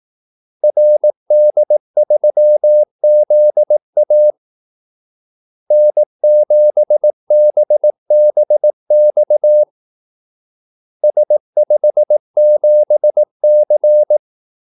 Morse code, heard as a keyed tone, about 18 wpm, 600 Hz. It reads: RD3ZA N7BBX S57C
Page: Kaldesignaler 18 wpm